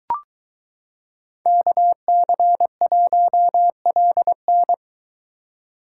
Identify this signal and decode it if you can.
Morse code, heard as a keyed tone, about 22 wpm, 700 Hz. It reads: KC1LN